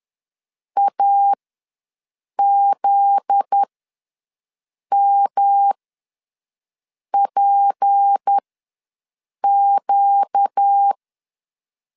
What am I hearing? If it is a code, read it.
Morse code, heard as a keyed tone, about 10 wpm, spaced as for 5 wpm, 800 Hz. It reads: AZMPQ
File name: LEZIONI / 10 SIGNS / 10A SLASH ws 10wpm 4121